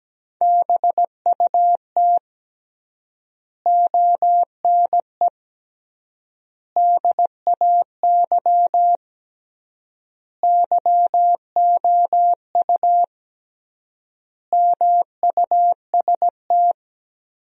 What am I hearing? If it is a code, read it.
Morse code, heard as a keyed tone, about 17 wpm, 700 Hz. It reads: BUT ONE DAY YOU MUST
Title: 3x Word-Spacing - Sentences from Top 400 Words - 17wpm